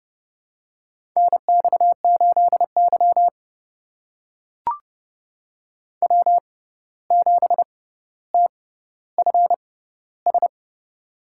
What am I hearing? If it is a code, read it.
Morse code, heard as a keyed tone, about 29 wpm, spaced as for 14 wpm, 700 Hz. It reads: NX8Y W7TFS
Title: USA Call Signs - 6x Character Spacing - 30wpm